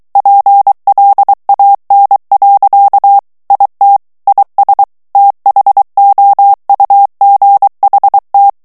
Morse code, heard as a keyed tone, about 23 wpm, 800 Hz. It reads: PLAN. IT IS THOUGHT
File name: morse mp3 / ANGLAIS texte clair 2 vitesse 22 mots minute